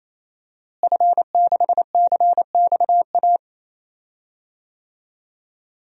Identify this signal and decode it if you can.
Morse code, heard as a keyed tone, about 28 wpm, 700 Hz. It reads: F6CXA
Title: DXCC-100 Call Signs - 5 Characters - Repeated 3x - 28wpm